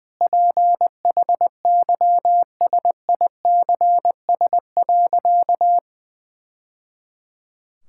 Morse code, heard as a keyed tone, about 21 wpm, 700 Hz. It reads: PHYSICS.
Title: Sentences from Top 1000 Words - Review Part 5 - 20wpm